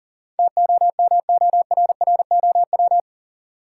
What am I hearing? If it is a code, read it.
Morse code, heard as a keyed tone, about 40 wpm, 700 Hz. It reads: TOMORROW
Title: Sentences from Top 1100 Words - New Word Drill - 40wpm